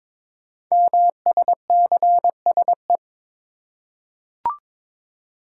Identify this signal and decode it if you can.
Morse code, heard as a keyed tone, about 22 wpm, 700 Hz. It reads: MSCSE
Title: Code Groups - 5 Letter:Numbers - 22wpm